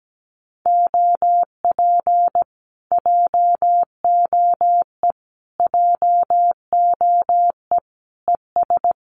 Morse code, heard as a keyed tone, about 17 wpm, 700 Hz. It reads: OP JOE JOE ES
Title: Common QSO Phrases Encore 17wpm